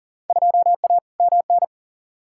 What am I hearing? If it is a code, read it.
Morse code, heard as a keyed tone, about 40 wpm, 700 Hz. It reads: 2A MN
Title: Field Day Exchanges Encore - 40wpm